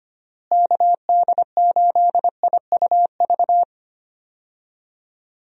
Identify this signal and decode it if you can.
Morse code, heard as a keyed tone, about 25 wpm, 700 Hz. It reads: KD8IUV
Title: USA Call Signs - 6 Characters - Repeated 2x - 25wpm